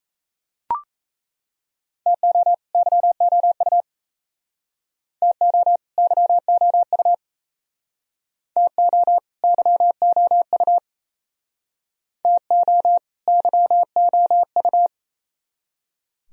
Morse code, heard as a keyed tone, about 37 wpm, 700 Hz. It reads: TO YOU TO YOU TO YOU TO YOU